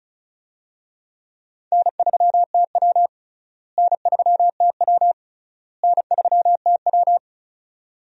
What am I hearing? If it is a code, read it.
Morse code, heard as a keyed tone, about 36 wpm, 700 Hz. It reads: N3TW N3TW N3TW
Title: USA Call Signs - 4 Characters - Repeated 3x - 35wpm